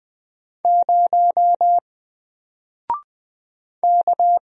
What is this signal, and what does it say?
Morse code, heard as a keyed tone, about 20 wpm, 700 Hz. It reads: 0 K
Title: Single Character (Letter:Number:Punctuation:Prosign) 20wpm